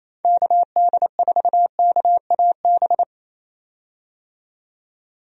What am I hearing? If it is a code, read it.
Morse code, heard as a keyed tone, about 27 wpm, 700 Hz. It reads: KD4KAB
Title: USA Call Signs - 6 Characters - 28wpm